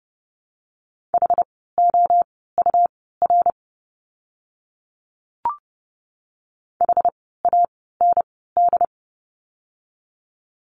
Morse code, heard as a keyed tone, about 30 wpm, spaced as for 17 wpm, 700 Hz. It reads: HOUR HAND
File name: course / 30:10wpm / 115 Lesson 19f - Letter w - Words 30wpm